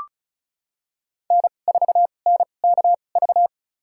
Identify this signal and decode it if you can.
Morse code, heard as a keyed tone, about 36 wpm, spaced as for 25 wpm, 700 Hz. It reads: N4NKV
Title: USA Call Signs - 5 Characters - 2x Character Spacing - 35wpm